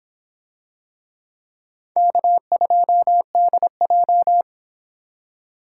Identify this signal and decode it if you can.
Morse code, heard as a keyed tone, about 22 wpm, 700 Hz. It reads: K2DJ